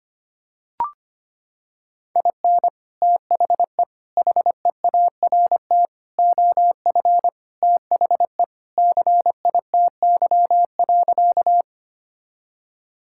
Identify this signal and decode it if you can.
Morse code, heard as a keyed tone, about 25 wpm, 700 Hz. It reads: IN THE HEART OF THE CITY.
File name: practice / 25wpm / Sentences with 3-Word Phrases - 25wpm